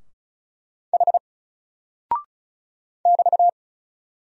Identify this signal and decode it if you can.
Morse code, heard as a keyed tone, about 34 wpm, 700 Hz. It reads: H <BT>
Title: Single Character (Letter:Number:Punctuation:Prosign) Alt 35wpm